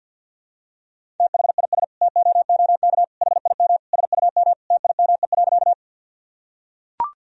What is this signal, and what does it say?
Morse code, heard as a keyed tone, about 45 wpm, 700 Hz. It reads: THIS TOOK HIM SUM TIME.
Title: Sentences from Top 300 Words and Abbreviations 50wpm